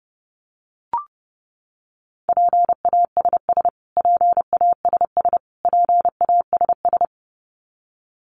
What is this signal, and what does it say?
Morse code, heard as a keyed tone, about 28 wpm, 700 Hz. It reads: PASS PASS PASS
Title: Top 500 Words Repeated 30wpm